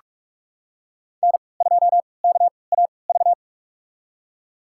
Morse code, heard as a keyed tone, about 46 wpm, spaced as for 26 wpm, 700 Hz. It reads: N2KAV
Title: USA Call Signs - 5 Characters - 3x Character Spacing - 45wpm